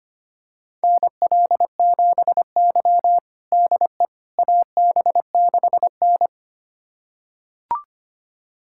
Morse code, heard as a keyed tone, about 25 wpm, 700 Hz. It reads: NL7Y DE AB6N
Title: USA Callsign DE Callsign - 4 Characters - 25wpm